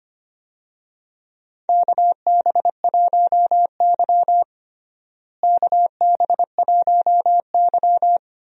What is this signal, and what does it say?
Morse code, heard as a keyed tone, about 25 wpm, 700 Hz. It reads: KB1Y KB1Y
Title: USA Call Signs - 4 Characters - Repeated 3x - 25wpm